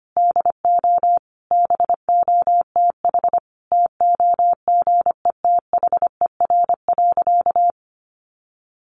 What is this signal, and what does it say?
Morse code, heard as a keyed tone, about 25 wpm, 700 Hz. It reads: DO BOTH TOGETHER.
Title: Sentences from Top 1300 Words 25wpm